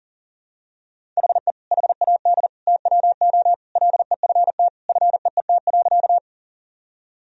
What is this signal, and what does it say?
Morse code, heard as a keyed tone, about 40 wpm, 700 Hz. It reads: HE HAD TWO LEFT FEET.
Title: Sentences from Top 1000 Words - Review Part 1 - 40wpm